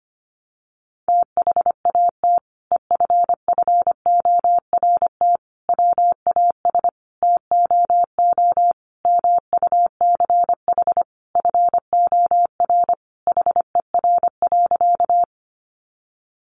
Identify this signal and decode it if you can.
Morse code, heard as a keyed tone, about 25 wpm, 700 Hz. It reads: THAT EFFORT WAS TOO MUCH FOR HER.
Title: Sentences from Top 1200 Words 25wpm